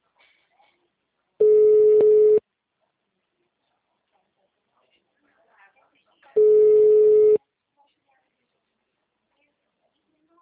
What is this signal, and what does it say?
Ringback tone (Europe, 425 Hz), with speech.
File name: out- -2009- - - .2401